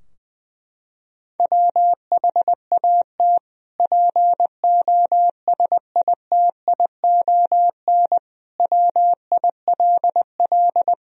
Morse code, heard as a keyed tone, about 21 wpm, 700 Hz. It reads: WHAT POSITION WILL
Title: Sentences from Top 1000 Words - Review Part 3 - 20wpm